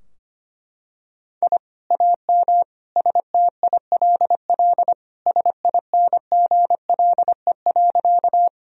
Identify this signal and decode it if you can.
Morse code, heard as a keyed tone, about 25 wpm, 700 Hz. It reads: I AM STILL SINGLE.
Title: Sentences from Top 1000 Words - Review Part 4 - 25wpm